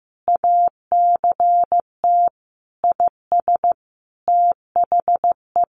Morse code, heard as a keyed tone, about 14 wpm, 700 Hz. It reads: ACT IS THE
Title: Sentences from Top 1400 Words 15wpm